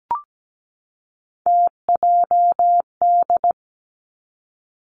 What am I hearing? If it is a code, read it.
Morse code, heard as a keyed tone, about 17 wpm, 700 Hz. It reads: TJD
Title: Three Challenging Letter-Number Combinations 17wpm